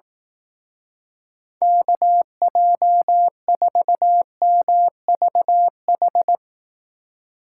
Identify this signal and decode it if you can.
Morse code, heard as a keyed tone, about 18 wpm, 700 Hz. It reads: KJ4MVH
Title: Speed Racing - USA Call Signs 15wpm